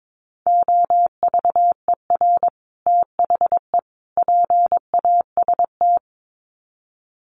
Morse code, heard as a keyed tone, about 22 wpm, 700 Hz. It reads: OVER THE PAST
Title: Sets of 3 Words 22wpm